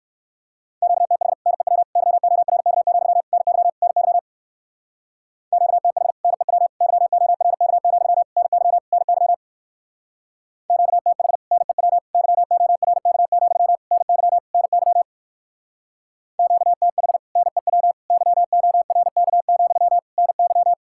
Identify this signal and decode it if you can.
Morse code, heard as a keyed tone, about 56 wpm, 700 Hz. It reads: QTH NEW YORK, NY NY QTH NEW YORK, NY NY QTH NEW YORK, NY NY QTH NEW YORK, NY
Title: Speed-Racing Common QSO Phrases Encore - 45wpm